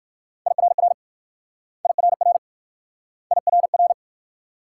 Morse code, heard as a keyed tone, about 56 wpm, 700 Hz. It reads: IRR IRR IRR
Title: Speed-Racing Multi-letter Phonemes - 45wpm